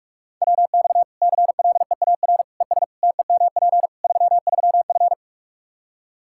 Morse code, heard as a keyed tone, about 46 wpm, 700 Hz. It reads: WX CLEAR ES TEMP 33F
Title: Practice QSO - Exchange 2 - Info - 45wpm